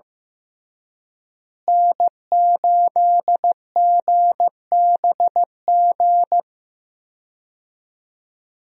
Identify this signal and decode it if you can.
Morse code, heard as a keyed tone, about 15 wpm, 700 Hz. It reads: N8GBG
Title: USA Call Signs - 5 Characters - Repeated 3x - 15wpm